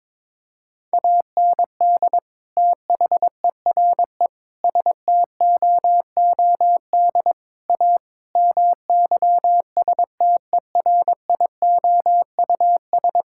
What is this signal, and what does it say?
Morse code, heard as a keyed tone, about 22 wpm, 700 Hz. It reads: AND THERE STOOD A MYSTERIOUS
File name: practice / 22wpm / Sentences with 2-Word Phrases - 22wpm